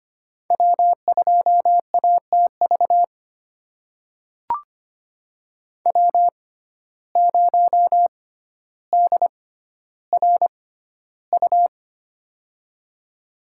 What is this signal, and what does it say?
Morse code, heard as a keyed tone, about 25 wpm, spaced as for 13 wpm, 700 Hz. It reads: W2ATV W0DRU
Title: USA Call Signs - 5 Characters - 6x Character Spacing - 25wpm